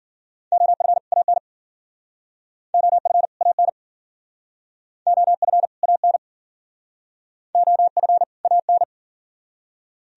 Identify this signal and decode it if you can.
Morse code, heard as a keyed tone, about 53 wpm, 700 Hz. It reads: OF AN OF AN OF AN OF AN